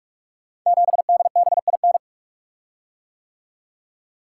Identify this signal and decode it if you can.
Morse code, heard as a keyed tone, about 46 wpm, 700 Hz. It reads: 7DBIN
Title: Code Groups - 5 Letter:Numbers - 45wpm